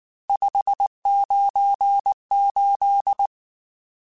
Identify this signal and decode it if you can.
Morse code, heard as a keyed tone, about 19 wpm, 800 Hz. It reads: 598
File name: RSTs / RST-mp3s / Various RST @18